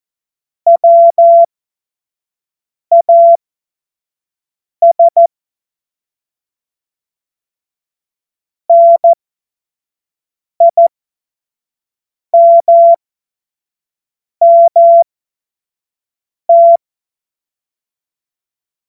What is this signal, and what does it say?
Morse code, heard as a keyed tone, about 14 wpm, spaced as for 5 wpm, 680 Hz. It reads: WAS NIMMT
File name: Morsekurs / Witze / Witz 21. Dezember 2021, 50 F70